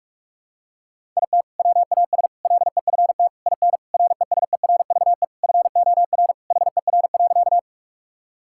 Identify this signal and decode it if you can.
Morse code, heard as a keyed tone, about 41 wpm, 700 Hz. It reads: IT WAS LEFT IN RESERVE FOR HER.
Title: Sentences from Top 1000 Words - Review Part 6 - 45wpm